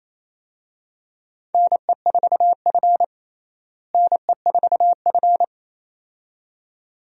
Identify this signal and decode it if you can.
Morse code, heard as a keyed tone, about 28 wpm, 700 Hz. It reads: NE4F NE4F
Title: USA Call Signs - 4 Characters - Repeated 2x - 28wpm